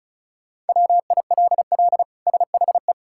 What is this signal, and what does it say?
Morse code, heard as a keyed tone, about 36 wpm, 700 Hz. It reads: WILL SHE
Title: Sentences from Top 1000 Words - Review Part 6 - 35wpm